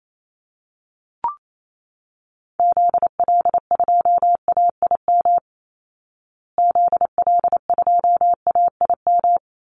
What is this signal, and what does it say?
Morse code, heard as a keyed tone, about 28 wpm, 700 Hz. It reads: ZL2AIM ZL2AIM
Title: Call Signs Repeated 2x - 28wpm